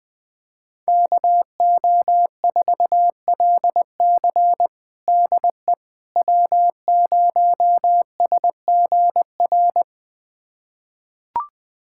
Morse code, heard as a keyed tone, about 20 wpm, 700 Hz. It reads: KO4LC DE W0SGR
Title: USA Callsign DE Callsign - 5 Characters - 20wpm